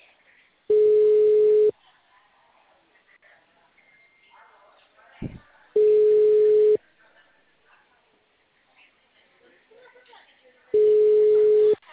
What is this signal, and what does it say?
Ringback tone (Europe, 425 Hz), with speech.